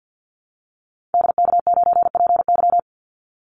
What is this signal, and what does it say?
Morse code, heard as a keyed tone, about 50 wpm, 700 Hz. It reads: DK9PY
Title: Call Signs - 50wpm